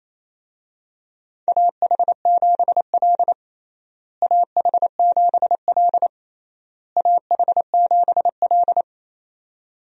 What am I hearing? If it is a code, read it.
Morse code, heard as a keyed tone, about 28 wpm, 700 Hz. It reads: AH7L AH7L AH7L
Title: USA Call Signs - 4 Characters - Repeated 3x - 28wpm